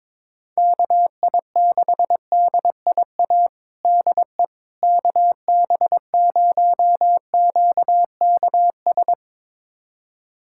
Morse code, heard as a keyed tone, about 22 wpm, 700 Hz. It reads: KI6DIA DE KB0QKS
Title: USA Callsign DE Callsign 22wpm